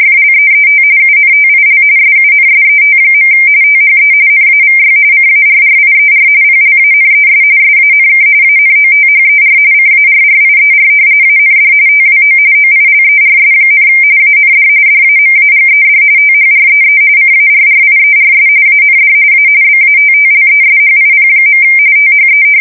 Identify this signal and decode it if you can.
RTTY, about 45.45 baud. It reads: FLEET UNTIL SUNSET. THE NAVY CARRIER PLANES SANK ANOTHER CARRIER AND TWO OILERS, AND DAMAGED THREE CARRIERS AND A BATTLESHIP. THE
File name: fsk170 / battleshipnj